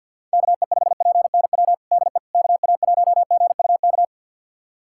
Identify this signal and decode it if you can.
Morse code, heard as a keyed tone, about 51 wpm, 700 Hz. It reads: KE5PNW DE KA1GUK
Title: USA Callsign DE Callsign - 6 Characters - 50wpm